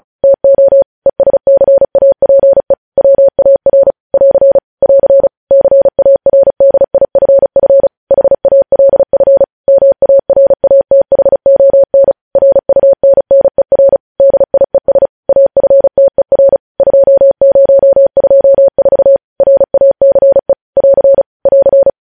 Morse code, heard as a keyed tone, about 36 wpm, 550 Hz. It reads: TO ESCAPE WAR <AR> <AR> CARDIFF HALF MARATHON RUNNER DIES AFTER 2024 RACE <AR> <AR>